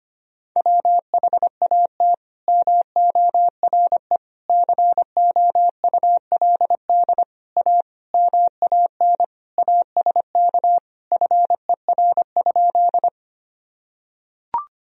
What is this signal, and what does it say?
Morse code, heard as a keyed tone, about 25 wpm, 700 Hz. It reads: WHAT MORE COULD A MAN ASK FER?
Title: Sentences from Top 200 Words and Abbreviations 25wpm